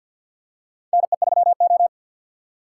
Morse code, heard as a keyed tone, about 52 wpm, 700 Hz. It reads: NE3O
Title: USA Call Signs - 4 Characters - 6x Character Spacing - 50wpm